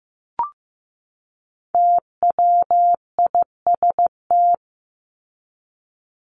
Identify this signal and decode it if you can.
Morse code, heard as a keyed tone, about 15 wpm, 700 Hz. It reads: TWIST
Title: Top 1751-2000 Words 15wpm